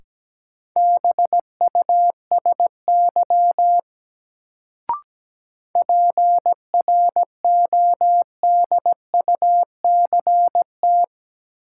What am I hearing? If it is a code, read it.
Morse code, heard as a keyed tone, about 17 wpm, 700 Hz. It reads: BUSY PRODUCT
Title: Sentences from Top 1000 Words - Review Part 2 - New Word Drill - 17wpm